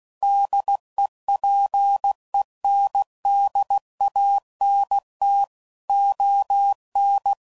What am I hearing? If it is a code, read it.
Morse code, heard as a keyed tone, about 16 wpm, 800 Hz. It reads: DEPENDANT ON
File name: CW Code Groups @15WPM / Technical Subjects / XL